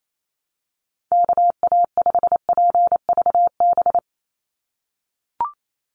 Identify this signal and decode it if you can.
Morse code, heard as a keyed tone, about 28 wpm, 700 Hz. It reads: KA5PVB
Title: NA SOTA Chaser Callsigns 28wpm